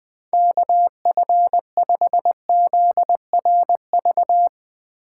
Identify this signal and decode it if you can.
Morse code, heard as a keyed tone, about 21 wpm, 700 Hz. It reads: KF5ZRV
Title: USA Call Signs - 6 Characters - 4x Character Spacing - 20wpm